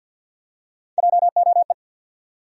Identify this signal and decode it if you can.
Morse code, heard as a keyed tone, about 48 wpm, 700 Hz. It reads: JOE
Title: USA Common Names - Max 3 Letters - 50wpm